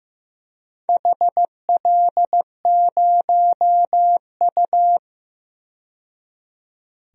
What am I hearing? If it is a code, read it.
Morse code, heard as a keyed tone, about 15 wpm, 700 Hz. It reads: HL0U
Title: DXCC-100 Call Signs - 4 Characters - 15wpm